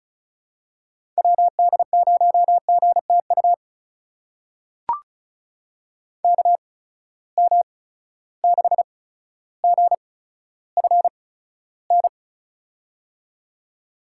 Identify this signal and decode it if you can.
Morse code, heard as a keyed tone, about 34 wpm, spaced as for 15 wpm, 700 Hz. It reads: WD0GTU KM6GFN
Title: USA Call Signs - 6 Characters - 8x Character Spacing - 35wpm